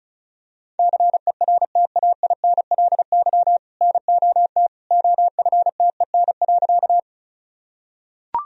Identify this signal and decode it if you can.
Morse code, heard as a keyed tone, about 34 wpm, 700 Hz. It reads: CERTAINLY NOT OFTEN.
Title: Sentences from Top 1000 Words - Review Part 1 - 35wpm